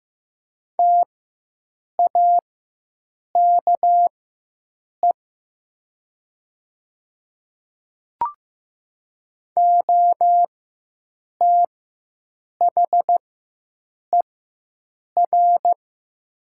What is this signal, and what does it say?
Morse code, heard as a keyed tone, about 16 wpm, spaced as for 7 wpm, 700 Hz. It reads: TAKE OTHER
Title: 4x Character Spacing - Top 300 Words - 15wpm